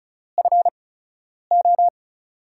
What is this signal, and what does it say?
Morse code, heard as a keyed tone, about 36 wpm, spaced as for 10 wpm, 700 Hz. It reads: FO
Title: USA Call Signs - 6 Characters - 8x Character Spacing - 35wpm